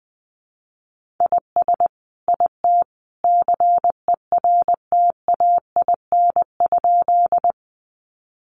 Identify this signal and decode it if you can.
Morse code, heard as a keyed tone, about 20 wpm, 700 Hz. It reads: IS IT CERTAIN?
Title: Sentences from Top 500 Words 20wpm